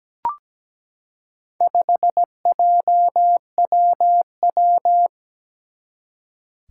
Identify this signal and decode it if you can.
Morse code, heard as a keyed tone, about 17 wpm, 700 Hz. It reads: 5JWW
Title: Four Challenging Letter-Number Combinations 17wpm